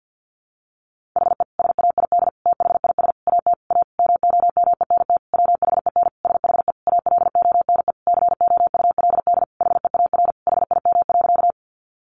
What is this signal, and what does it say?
Morse code, heard as a keyed tone, about 50 wpm, 700 Hz. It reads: HE SAID THIS AT A MOMENT WHEN SHE ALONE COULD HEAR HIM.
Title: Sentences from Top 1000 Words 50wpm